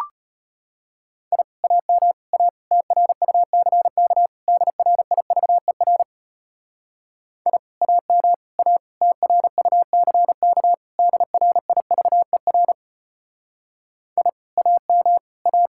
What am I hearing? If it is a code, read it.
Morse code, heard as a keyed tone, about 31 wpm, 700 Hz. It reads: I AM A TRUCK DRIVER I AM A TRUCK DRIVER I AM A
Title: Speed-Racing Common QSO Phrases Encore - 25wpm